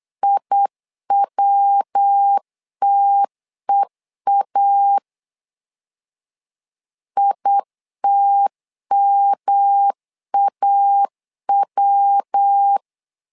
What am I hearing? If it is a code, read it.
Morse code, heard as a keyed tone, about 8 wpm, 800 Hz. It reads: IWTEA ITMAW
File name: LEZIONI / 3 MWRO / 3 MW 8wpm 2501